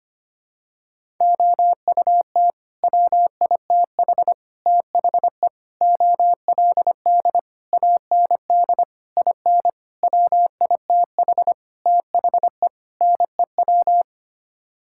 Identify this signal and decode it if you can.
Morse code, heard as a keyed tone, about 25 wpm, 700 Hz. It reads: OUT WITH THE OLD AND IN WITH THE NEW
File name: Sets of 9 Words - 25wpm